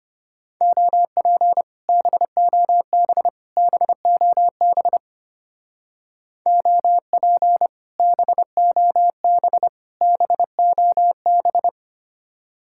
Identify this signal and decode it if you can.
Morse code, heard as a keyed tone, about 30 wpm, 700 Hz. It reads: OP BOB BOB OP BOB BOB
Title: Speed-Racing Common QSO Phrases - 25wpm